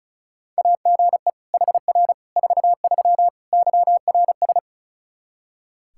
Morse code, heard as a keyed tone, about 34 wpm, 700 Hz. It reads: AGE HR 43 YRS
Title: Practice QSO - Exchange 3 - More Info - 35wpm